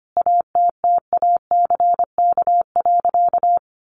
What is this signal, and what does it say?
Morse code, heard as a keyed tone, about 25 wpm, 700 Hz. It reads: ATTACK.
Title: Sentences from Top 1500 Words 25wpm